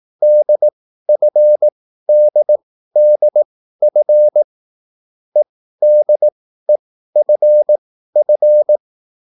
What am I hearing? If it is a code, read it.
Morse code, heard as a keyed tone, about 18 wpm, spaced as for 13 wpm, 600 Hz. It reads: DFDDF EDEFF